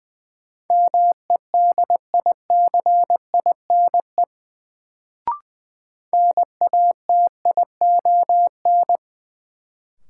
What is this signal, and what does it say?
Morse code, heard as a keyed tone, about 21 wpm, 700 Hz. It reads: MEDICINE NATION
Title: Sentences from Top 900 Words - New Word Drill - 20wpm